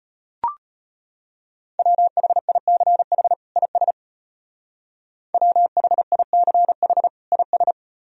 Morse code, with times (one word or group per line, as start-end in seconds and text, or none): 1.8-3.3 WHICH
3.6-3.9 IS
5.3-7.1 WHICH
7.3-7.7 IS